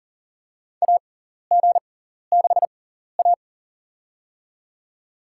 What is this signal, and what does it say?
Morse code, heard as a keyed tone, about 40 wpm, spaced as for 14 wpm, 700 Hz. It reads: AG6A
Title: USA Call Signs - 4 Characters - 6x Character Spacing - 40wpm